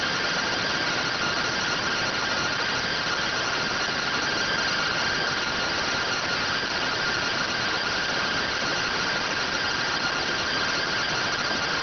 Two-tone teleprinter signal, about 170 Hz shift.